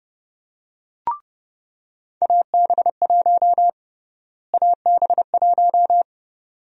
Morse code, heard as a keyed tone, about 32 wpm, 700 Hz. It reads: AB1 AB1
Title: USA Call Sign Prefixes - Repeated 2x - 30wpm